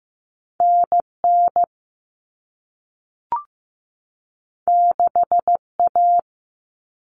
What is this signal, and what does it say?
Morse code, heard as a keyed tone, about 15 wpm, 700 Hz. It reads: NN 6A
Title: All Cut Numbers - 2-Digit 15wpm